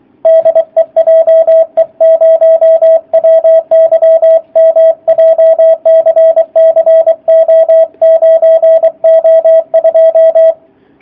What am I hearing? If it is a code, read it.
Morse code, heard as a keyed tone, about 23 wpm, 650 Hz, with radio noise. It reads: DE JE0WYM JCC O9O2
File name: jh8cbh / 2019cw / sp27- w